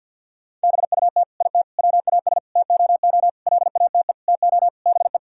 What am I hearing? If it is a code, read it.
Morse code, heard as a keyed tone, about 50 wpm, 700 Hz. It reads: BUT IT WAS TOO LATE TO BE
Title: Sentences from Top 1600 Words 50wpm